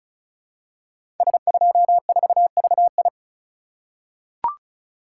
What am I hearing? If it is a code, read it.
Morse code, heard as a keyed tone, about 36 wpm, 700 Hz. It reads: S24VI
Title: Code Groups - 5 Letter:Numbers - 35wpm